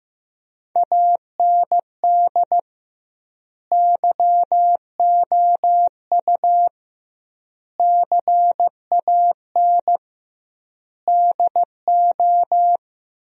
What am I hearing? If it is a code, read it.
Morse code, heard as a keyed tone, about 15 wpm, 700 Hz. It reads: AND YOU CAN DO
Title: 2x Word-Spacing - Sentences from Top 500 Words - 15wpm